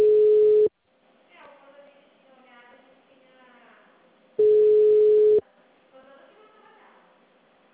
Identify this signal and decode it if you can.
Ringback tone (Europe, 425 Hz), with speech.
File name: out- -2007- - - .1379